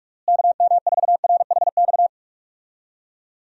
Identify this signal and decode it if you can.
Morse code, heard as a keyed tone, about 46 wpm, 700 Hz. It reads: KM4RHX
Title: USA Call Signs - 6 Characters - Repeated 3x - 45wpm